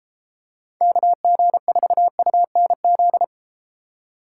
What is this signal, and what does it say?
Morse code, heard as a keyed tone, about 33 wpm, 700 Hz. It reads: KG4UNZ